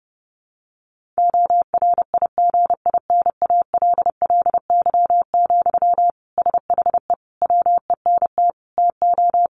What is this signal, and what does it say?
Morse code, heard as a keyed tone, about 30 wpm, 700 Hz. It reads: ORIGINALLY, SHE WENT TO
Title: Sentences from Top 1500 Words 30wpm